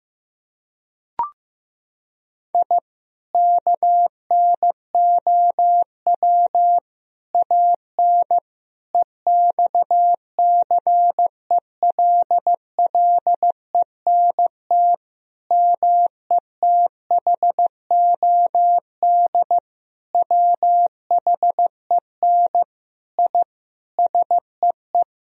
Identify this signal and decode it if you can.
Morse code, heard as a keyed tone, about 15 wpm, 700 Hz. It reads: I KNOW AN EXCELLENT METHOD WHEN I SEE
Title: Sentences from Top 1000 Words - Review Part 3 - 15wpm